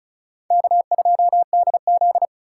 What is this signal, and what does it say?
Morse code, heard as a keyed tone, about 34 wpm, 700 Hz. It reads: K2DZ
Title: USA Call Signs - 3x Character Spacing - 35wpm